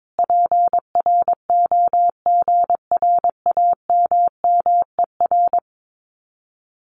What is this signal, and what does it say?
Morse code, heard as a keyed tone, about 22 wpm, 700 Hz. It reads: PROGRAMMER
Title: Common QSO Phrases Encore 22wpm